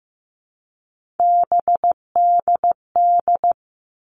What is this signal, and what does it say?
Morse code, heard as a keyed tone, about 15 wpm, 700 Hz. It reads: BDD
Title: All Cut Numbers - 3-Digit 15wpm